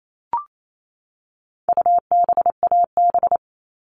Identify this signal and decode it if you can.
Morse code, heard as a keyed tone, about 28 wpm, 700 Hz. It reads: UBAB
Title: All Cut Numbers - 4-Digit 28wpm